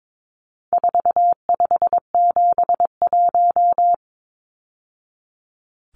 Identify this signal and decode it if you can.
Morse code, heard as a keyed tone, about 22 wpm, 700 Hz. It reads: 4571
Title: Numbers - 4-Digit 22wpm